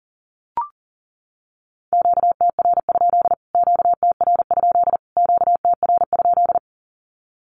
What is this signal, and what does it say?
Morse code, heard as a keyed tone, about 40 wpm, 700 Hz. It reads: QTR? QTR? QTR?
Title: Q-Codes Repeated 3x 40wpm